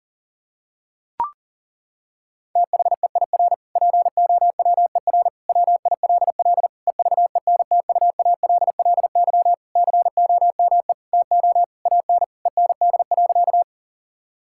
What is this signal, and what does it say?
Morse code, heard as a keyed tone, about 40 wpm, 700 Hz. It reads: THEIR POWER WILL EVENTUALLY COME TO AN END.
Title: Sentences from Top 1000 Words - Review Part 6 - 40wpm